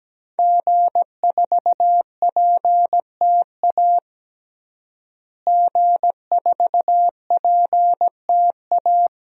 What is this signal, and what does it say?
Morse code, heard as a keyed tone, about 17 wpm, 700 Hz. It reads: G4PTA G4PTA
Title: DXCC-100 Call Signs - 5 Characters - Repeated 3x - 17wpm